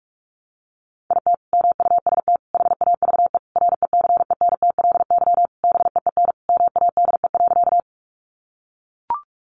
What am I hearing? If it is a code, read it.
Morse code, heard as a keyed tone, about 44 wpm, 700 Hz. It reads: IT MUST HAVE RECENTLY BEEN MADE.
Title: Sentences from Top 1200 Words 45wpm